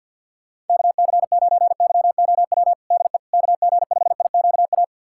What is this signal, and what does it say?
Morse code, heard as a keyed tone, about 51 wpm, 700 Hz. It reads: KC9YOW DE KG5IQA
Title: USA Callsign DE Callsign - 6 Characters - 50wpm